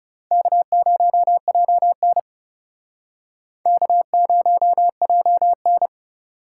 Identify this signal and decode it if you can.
Morse code, heard as a keyed tone, about 35 wpm, 700 Hz. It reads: K0JN K0JN
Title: Speed Racing - USA 1x2 Call Signs 30wpm